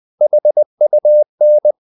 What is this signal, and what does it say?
Morse code, heard as a keyed tone, about 20 wpm, 600 Hz. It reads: HUN
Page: Ferie 20 wpm